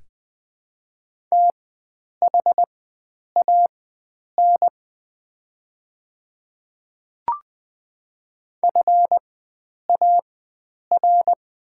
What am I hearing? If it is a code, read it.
Morse code, heard as a keyed tone, about 20 wpm, spaced as for 9 wpm, 690 Hz. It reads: THAN FAR
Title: 4x Character Spacing - Top 300 Words - 20wpm